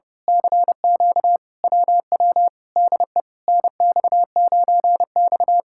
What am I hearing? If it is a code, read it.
Morse code, heard as a keyed tone, about 30 wpm, 700 Hz. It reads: CQ WW DE NX9X
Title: Speed-Racing Common QSO Phrases - 30wpm